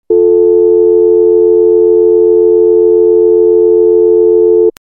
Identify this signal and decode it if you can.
Dial tone (North America), 350 + 440 Hz.
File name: xp-80 / patches / Dial Tone 1.1